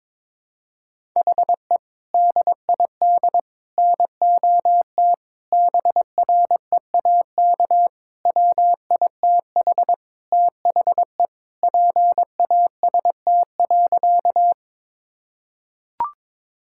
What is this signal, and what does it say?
Morse code, heard as a keyed tone, about 22 wpm, 700 Hz. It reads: HE DID NOT BREAK WITH THE PAST.
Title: Sentences from Top 1000 Words - Review Part 6 - 22wpm